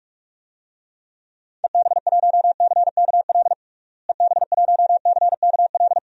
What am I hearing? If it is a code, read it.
Morse code, heard as a keyed tone, about 41 wpm, 700 Hz. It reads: EB1CKL EB1CKL
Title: DXCC-100 Call Signs - 6 Characters - Repeated 2x - 45wpm